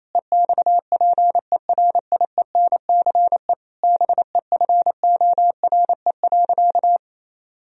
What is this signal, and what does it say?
Morse code, heard as a keyed tone, about 28 wpm, 700 Hz. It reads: EXPERIENCE BEFORE.
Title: Sentences from Top 1000 Words - Review Part 3 - 28wpm